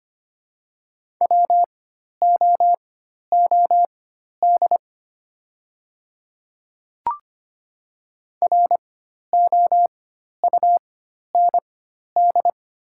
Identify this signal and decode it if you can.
Morse code, heard as a keyed tone, about 25 wpm, spaced as for 12 wpm, 700 Hz. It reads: WOOD ROUND
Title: 4x Character Spacing - Top 300 Words - 25wpm